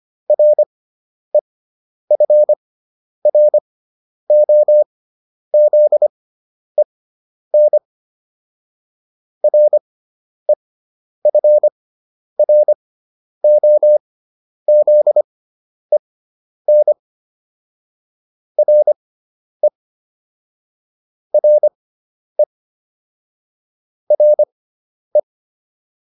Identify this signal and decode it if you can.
Morse code, heard as a keyed tone, about 25 wpm, spaced as for 10 wpm, 600 Hz. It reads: REFROZEN REFROZEN RE RE RE